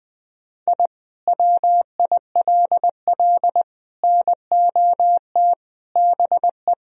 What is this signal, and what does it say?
Morse code, heard as a keyed tone, about 21 wpm, 700 Hz. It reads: I WILL NOT BE
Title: Sentences from Top 1000 Words - Review Part 1 - 20wpm